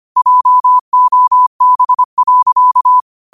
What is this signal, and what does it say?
Morse code, heard as a keyed tone, about 25 wpm, 1000 Hz. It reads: JOB.
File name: qotdfiles / 25wpm